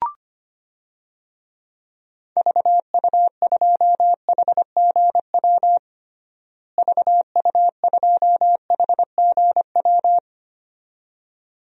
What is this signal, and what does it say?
Morse code, heard as a keyed tone, about 25 wpm, 700 Hz. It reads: VU2HGW VU2HGW
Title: DXCC-100 Call Signs - 6 Characters - Repeated 2x - 25wpm